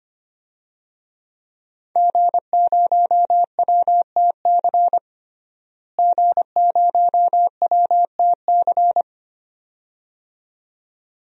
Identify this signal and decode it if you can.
Morse code, heard as a keyed tone, about 25 wpm, 700 Hz. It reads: G0WTC G0WTC
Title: DXCC-100 Call Signs - 5 Characters - Repeated 2x - 25wpm